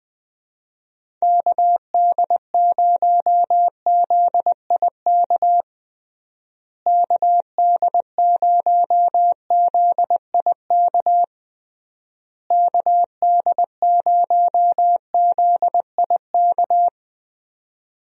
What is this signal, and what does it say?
Morse code, heard as a keyed tone, about 21 wpm, 700 Hz. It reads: KD0ZIK KD0ZIK KD0ZIK
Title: USA Call Signs - 6 Characters - Repeated 3x - 20wpm